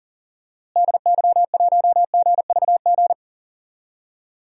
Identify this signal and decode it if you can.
Morse code, heard as a keyed tone, about 39 wpm, 700 Hz. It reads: DY1GVG
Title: DXCC-100 Call Signs - 6 Characters - Repeated 2x - 40wpm